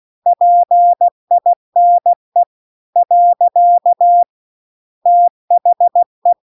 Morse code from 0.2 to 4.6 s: PINE .